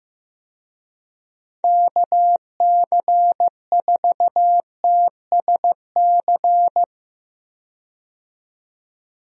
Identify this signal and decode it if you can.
Morse code, heard as a keyed tone, about 15 wpm, 700 Hz. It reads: KC4TSC
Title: USA Call Signs - 6 Characters - 15wpm